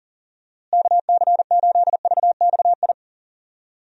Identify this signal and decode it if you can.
Morse code, heard as a keyed tone, about 40 wpm, 700 Hz. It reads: KC8VXI
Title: USA Call Signs - 6 Characters - Repeated 3x - 40wpm